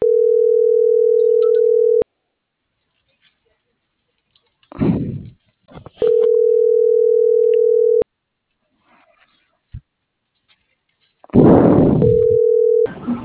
Ringback tone (North America), 440 + 480 Hz.